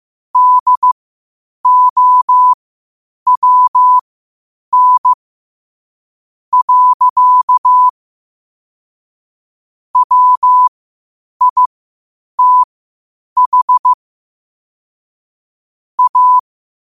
Morse code, heard as a keyed tone, about 15 wpm, spaced as for 8 wpm, 1000 Hz. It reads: DOWN. WITH A
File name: qotdfiles / 8wpm